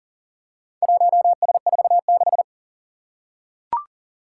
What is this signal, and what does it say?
Morse code, heard as a keyed tone, about 41 wpm, 700 Hz. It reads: 1S46
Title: Four Challenging Letter-Number Combinations 40wpm